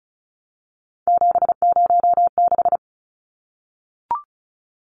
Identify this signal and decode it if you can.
Morse code, heard as a keyed tone, about 34 wpm, 700 Hz. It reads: 706
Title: Numbers - 3-Digit 35wpm